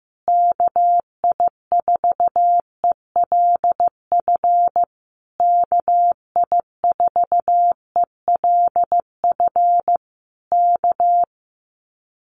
Morse code, heard as a keyed tone, about 14 wpm, 700 Hz. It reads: KI4ELF KI4ELF K
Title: Common QSO Phrases Encore 15wpm